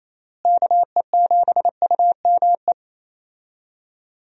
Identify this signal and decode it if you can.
Morse code, heard as a keyed tone, about 28 wpm, 700 Hz. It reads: KE7UME
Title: USA Call Signs - 6 Characters - Repeated 2x - 28wpm